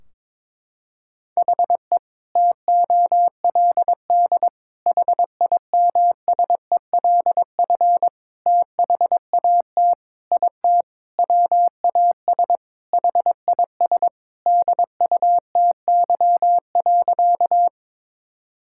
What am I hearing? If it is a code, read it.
Morse code, heard as a keyed tone, about 22 wpm, 700 Hz. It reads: HE TOLD HIMSELF THAT IT WAS HIS DUTY.
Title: Sentences from Top 1600 Words 22wpm